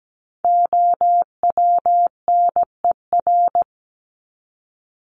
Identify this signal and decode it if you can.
Morse code, heard as a keyed tone, about 17 wpm, 700 Hz. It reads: OWNER
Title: Top 1501-1750 Words 17wpm